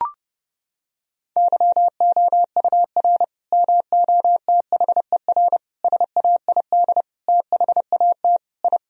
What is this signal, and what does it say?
Morse code, heard as a keyed tone, about 29 wpm, 700 Hz. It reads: YOUR MOTHER SAID THAT I
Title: Sentences from Top 1000 Words - Review Part 2 - 30wpm